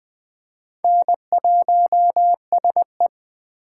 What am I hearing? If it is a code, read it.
Morse code, heard as a keyed tone, about 20 wpm, 700 Hz. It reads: N1SE